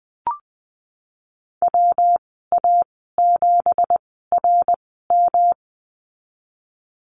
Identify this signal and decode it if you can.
Morse code, heard as a keyed tone, about 19 wpm, spaced as for 14 wpm, 700 Hz. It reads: WA7RM
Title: 179 Lesson 29g - Slash - Call Signs 20wpm